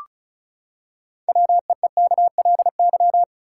Morse code, heard as a keyed tone, about 36 wpm, 700 Hz. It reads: WEEKLY
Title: Six Letter Words - 35wpm